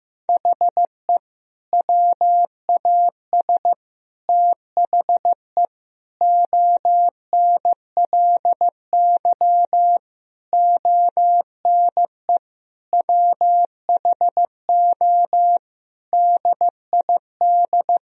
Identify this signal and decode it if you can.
Morse code, heard as a keyed tone, about 16 wpm, 700 Hz. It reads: HE WAS THE ONLY ONE WHO DID
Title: Sentences from Top 1000 Words - Review Part 3 - 15wpm